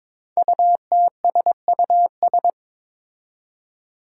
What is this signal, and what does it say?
Morse code, heard as a keyed tone, about 22 wpm, 700 Hz. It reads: UTSUS
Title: Five Challenging Letter-Number Combinations 22wpm